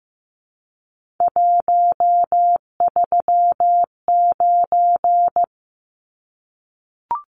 Morse code, heard as a keyed tone, about 15 wpm, 700 Hz. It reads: 139
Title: Numbers - 3-Digit 15wpm